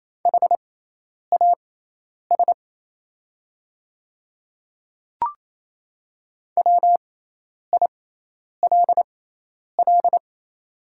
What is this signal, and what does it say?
Morse code, heard as a keyed tone, about 28 wpm, spaced as for 10 wpm, 690 Hz. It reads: HAS WILL
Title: 6x Character Spacing - Top 100 Words - 28wpm